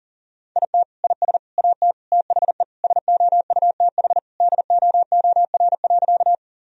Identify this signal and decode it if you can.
Morse code, heard as a keyed tone, about 40 wpm, 700 Hz. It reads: IT IS AT THE SOUTH DOOR.
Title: Sentences from Top 1000 Words - Review Part 2 - 40wpm